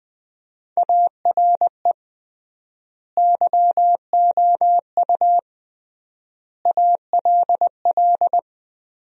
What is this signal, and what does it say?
Morse code, heard as a keyed tone, about 20 wpm, 700 Hz. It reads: ARE YOU ALL
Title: 3x Word-Spacing - Sentences from Top 400 Words - 20wpm